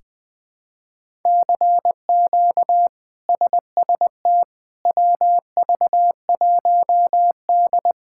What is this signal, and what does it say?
Morse code, heard as a keyed tone, about 21 wpm, 700 Hz. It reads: CQ SST WV1D
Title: K1USN SST QSO Practice - 20wpm